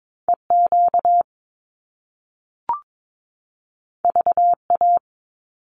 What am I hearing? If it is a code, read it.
Morse code, heard as a keyed tone, about 22 wpm, 700 Hz. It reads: EQ VA
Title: Top Two Letter Combinations in Words 22wpm